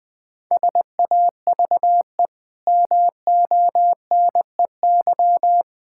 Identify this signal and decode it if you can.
Morse code, heard as a keyed tone, about 20 wpm, 700 Hz. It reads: SAVE MONEY
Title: Easy Sentences with Hints - 20wpm